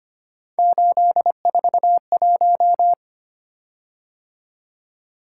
Morse code, heard as a keyed tone, about 25 wpm, 700 Hz. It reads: 841
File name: Three Letter-Number Combinations - 25wpm